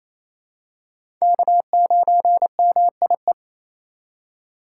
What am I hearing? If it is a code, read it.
Morse code, heard as a keyed tone, about 28 wpm, 700 Hz. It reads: K9MIE
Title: USA Call Signs - 5 Characters - 6x Character Spacing - 28wpm